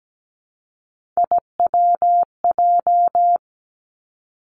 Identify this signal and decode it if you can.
Morse code, heard as a keyed tone, about 17 wpm, 700 Hz. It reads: IWJ
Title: Three Challenging Letter-Number Combinations 17wpm